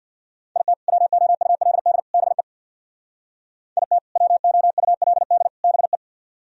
Morse code, heard as a keyed tone, about 64 wpm, 700 Hz. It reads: IT WOULD BE IT WOULD BE